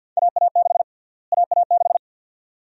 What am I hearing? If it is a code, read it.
Morse code, heard as a keyed tone, about 49 wpm, 700 Hz. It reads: AA6 AA6
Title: USA Call Sign Prefixes - Repeated 2x - 50wpm